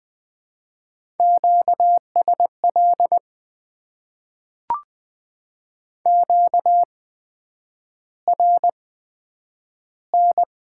Morse code, heard as a keyed tone, about 20 wpm, spaced as for 9 wpm, 700 Hz. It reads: QSL QRN
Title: Top 90 QSO Elements - 8x Character Spacing - 20wpm